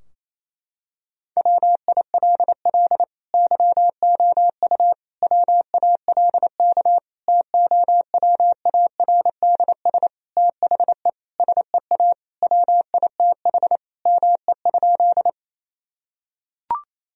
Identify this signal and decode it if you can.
Morse code, heard as a keyed tone, about 27 wpm, 700 Hz. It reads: WILL YOU WALK TOWARDS THE SEA WITH ME?
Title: Sentences from Top 1000 Words - Review Part 2 - 28wpm